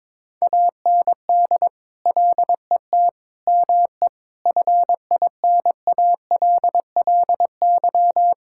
Morse code, heard as a keyed tone, about 22 wpm, 700 Hz. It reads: AND LET ME FINALLY
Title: Sentences from Top 1600 Words 22wpm